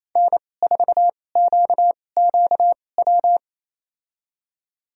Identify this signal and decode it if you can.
Morse code, heard as a keyed tone, about 28 wpm, spaced as for 20 wpm, 700 Hz. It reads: N4QQW
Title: USA Call Signs - 5 Characters - 2x Character Spacing - 28wpm